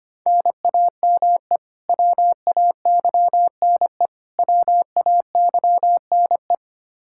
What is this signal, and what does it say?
Morse code, heard as a keyed tone, about 25 wpm, 700 Hz. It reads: NAME WAYNE WAYNE
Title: Practice QSO - Exchange 1 - Beginning - 25wpm